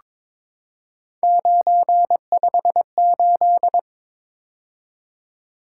Morse code, heard as a keyed tone, about 23 wpm, 700 Hz. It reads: 958
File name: practice / 22wpm / Three Letter-Number Combinations - 22wpm